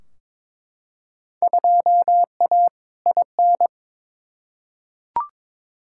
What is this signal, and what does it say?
Morse code, heard as a keyed tone, about 21 wpm, 700 Hz. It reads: 2A IN
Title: Field Day Exchanges - 22wpm